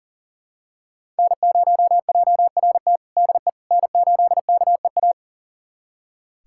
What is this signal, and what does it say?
Morse code, heard as a keyed tone, about 40 wpm, 700 Hz. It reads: N0JRT DE N8KEA
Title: USA Callsign DE Callsign - 5 Characters - 40wpm